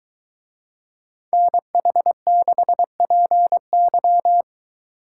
Morse code, heard as a keyed tone, about 23 wpm, 700 Hz. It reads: NH6PY